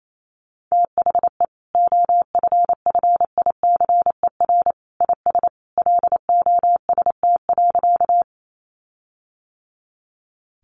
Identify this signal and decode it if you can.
Morse code, heard as a keyed tone, about 28 wpm, 700 Hz. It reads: THE OFFICER IS LOST.
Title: Sentences from Top 1400 Words 28wpm